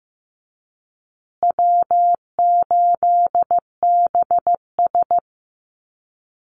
Morse code, heard as a keyed tone, about 15 wpm, 700 Hz. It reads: W8BS
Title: USA Call Signs Repeated 2x 15wpm